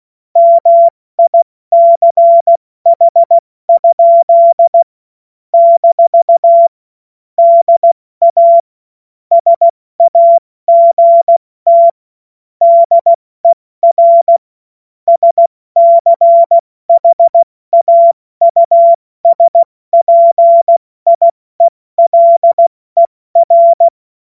Morse code, heard as a keyed tone, about 16 wpm, 680 Hz. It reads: MICH? - DA SAGT DER SCHAUSPIELER
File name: Morsekurs / Witze / Witz 24. August 2021, 70 F80